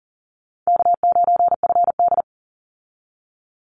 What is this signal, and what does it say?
Morse code, heard as a keyed tone, about 40 wpm, 700 Hz. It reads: K9FD
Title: Easy Call Signs - 40wpm